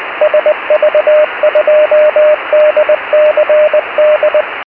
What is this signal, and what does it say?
Morse code, heard as a keyed tone, about 20 wpm, 610 Hz, with radio noise. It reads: SV2DCD
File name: 70mhz / sv2dcd pe1itr 70200khz